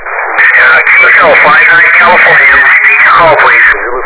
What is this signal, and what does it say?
Emergency Alert System (SAME) tones, heard as harsh buzzing bursts, with speech. It is heard from 0.5 to 3.0 s.